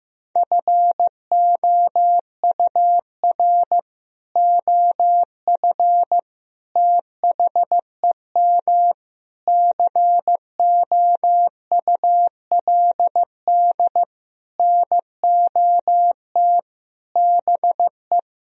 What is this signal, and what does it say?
Morse code, heard as a keyed tone, about 15 wpm, 700 Hz. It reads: FOUR OF THEM COULD NOT BE
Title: Sentences from Top 1000 Words - Review Part 1 - 15wpm